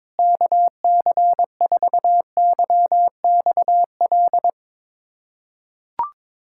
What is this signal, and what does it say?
Morse code, heard as a keyed tone, about 22 wpm, 700 Hz. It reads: KC4YXL
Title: USA Callsign DE Callsign 22wpm